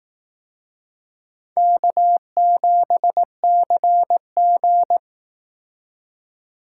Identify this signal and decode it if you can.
Morse code, heard as a keyed tone, about 18 wpm, 700 Hz. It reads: K7CG
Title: Speed Racing - USA 1x2 Call Signs 15wpm